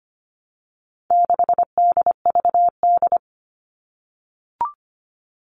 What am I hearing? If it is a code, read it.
Morse code, heard as a keyed tone, about 25 wpm, 700 Hz. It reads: 6DVD
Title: All Cut Numbers - 4-Digit 25wpm